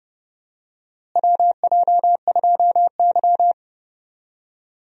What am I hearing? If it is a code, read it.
Morse code, heard as a keyed tone, about 30 wpm, 700 Hz. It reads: WJ2Y
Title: USA Call Signs - 4 Characters - Repeated 2x - 30wpm